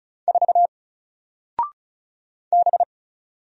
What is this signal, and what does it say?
Morse code, heard as a keyed tone, about 35 wpm, 700 Hz. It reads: 4 B
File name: Single-Letter-Number-35wpm